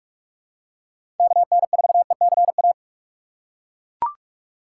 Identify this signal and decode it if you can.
Morse code, heard as a keyed tone, about 41 wpm, 700 Hz. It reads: KN4ECA